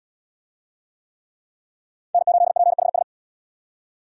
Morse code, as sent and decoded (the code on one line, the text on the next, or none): -. ---.. .--. .... ...
N8PHS